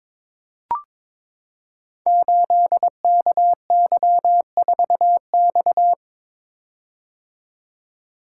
Morse code, heard as a keyed tone, about 22 wpm, 700 Hz. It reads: 8KY4X
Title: Code Groups - 5 Letter:Numbers - 22wpm